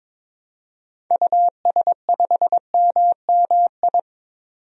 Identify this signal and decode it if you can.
Morse code, heard as a keyed tone, about 22 wpm, 700 Hz. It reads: US5MMI
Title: DXCC-100 Call Signs - 6 Characters - 22wpm